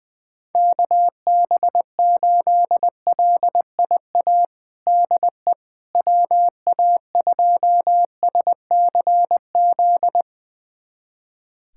Morse code, heard as a keyed tone, about 21 wpm, 700 Hz. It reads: KB8LIA DE WA2SCZ
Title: USA Callsign DE Callsign - 6 Characters - 20wpm